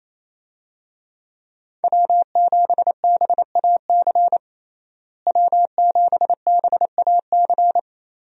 Morse code, heard as a keyed tone, about 28 wpm, 700 Hz. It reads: W7BAC W7BAC
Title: USA Call Signs - 5 Characters - Repeated 3x - 28wpm